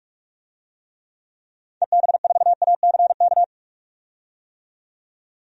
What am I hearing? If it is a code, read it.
Morse code, heard as a keyed tone, about 42 wpm, 700 Hz. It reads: EB4ACK
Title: DXCC-100 Call Signs - 6 Characters - 45wpm